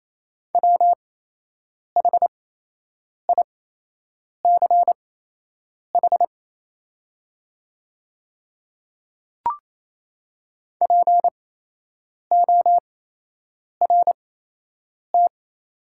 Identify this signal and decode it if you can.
Morse code, heard as a keyed tone, about 28 wpm, spaced as for 8 wpm, 700 Hz. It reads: WHICH PORT
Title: 8x Character Spacing - Top 300 Words - 28wpm